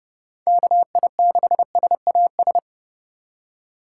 Morse code, heard as a keyed tone, about 31 wpm, 700 Hz. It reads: KI6SAS
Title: USA Call Signs - 6 Characters - 4x Character Spacing - 30wpm